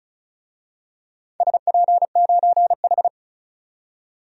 Morse code, heard as a keyed tone, about 36 wpm, 700 Hz. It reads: SP9H
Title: DXCC-100 Call Signs - 4 Characters - 35wpm